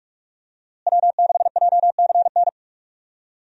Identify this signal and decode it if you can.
Morse code, heard as a keyed tone, about 47 wpm, 700 Hz. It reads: W6JCN
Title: USA Call Signs - 5 Characters - Repeated 2x - 45wpm